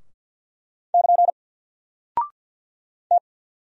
Morse code, heard as a keyed tone, about 48 wpm, 700 Hz. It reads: <KN> T
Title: Single Character (Letter:Number:Punctuation:Prosign) 50wpm